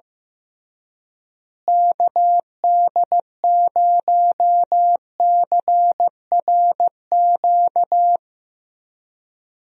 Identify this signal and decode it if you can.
Morse code, heard as a keyed tone, about 15 wpm, 700 Hz. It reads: KD0CRQ
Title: USA Call Signs - 6 Characters - Repeated 3x - 15wpm